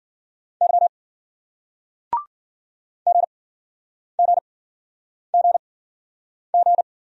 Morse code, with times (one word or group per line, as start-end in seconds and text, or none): 0.6-0.9 Y
3.1-6.8 GGGG